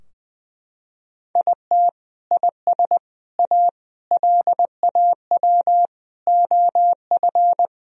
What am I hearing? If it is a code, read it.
Morse code, heard as a keyed tone, about 20 wpm, 700 Hz. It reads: IT IS A LAW OF
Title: Sentences from Top 1000 Words - Review Part 3 - 20wpm